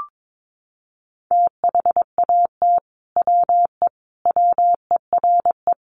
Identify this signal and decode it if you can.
Morse code, heard as a keyed tone, about 22 wpm, 700 Hz. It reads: THAT WE WERE
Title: Sets of 5 Words Encore2 22wpm